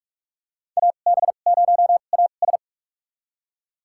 Morse code, heard as a keyed tone, about 42 wpm, spaced as for 31 wpm, 690 Hz. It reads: AB0AS
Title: USA Call Signs - 5 Characters - 2x Character Spacing - 45wpm